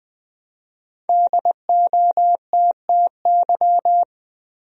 Morse code, heard as a keyed tone, about 20 wpm, 700 Hz. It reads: DOTTY
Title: USA Common Names - Max 5 Letters - 20wpm